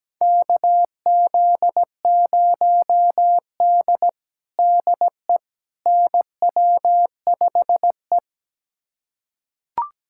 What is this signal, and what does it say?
Morse code, heard as a keyed tone, about 17 wpm, 700 Hz. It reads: KZ0D DE NW5E
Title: USA Callsign DE Callsign - 4 Characters - 17wpm